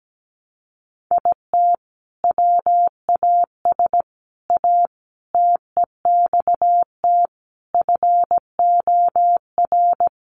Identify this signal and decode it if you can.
Morse code, heard as a keyed tone, about 17 wpm, 700 Hz. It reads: IT WAS A TEXT FOR
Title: Sentences from Top 1200 Words 17wpm